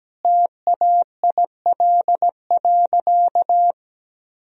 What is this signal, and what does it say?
Morse code, heard as a keyed tone, about 17 wpm, 700 Hz. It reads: TAIL.
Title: Sentences from Top 1000 Words - Review Part 3 - 17wpm